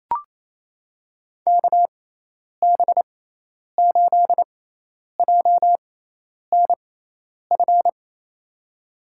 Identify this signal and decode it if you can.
Morse code, heard as a keyed tone, about 28 wpm, spaced as for 10 wpm, 700 Hz. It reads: KB8JNF
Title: USA Call Signs - 6 Characters - 6x Character Spacing - 28wpm